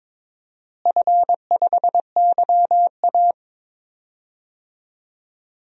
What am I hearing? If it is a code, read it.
Morse code, heard as a keyed tone, about 22 wpm, 700 Hz. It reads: F5YA
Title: DXCC-100 Call Signs - 4 Characters - Repeated 3x - 22wpm